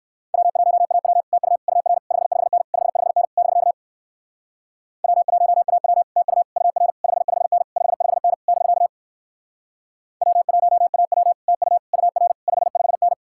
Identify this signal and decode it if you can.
Morse code, heard as a keyed tone, about 60 wpm, 700 Hz. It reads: W1AW TU UR 55N 55N <BK> W1AW TU UR 55N 55N <BK> W1AW TU UR 55N